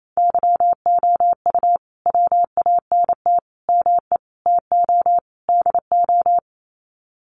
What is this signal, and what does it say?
Morse code, heard as a keyed tone, about 28 wpm, 700 Hz. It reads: YOU WANT ME TO DO
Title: Sets of 5 Words Encore2 28wpm